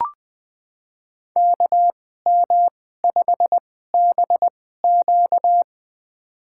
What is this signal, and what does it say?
Morse code, heard as a keyed tone, about 20 wpm, spaced as for 14 wpm, 700 Hz. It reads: KM5BQ
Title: USA Call Signs - 5 Characters - 2x Character Spacing - 20wpm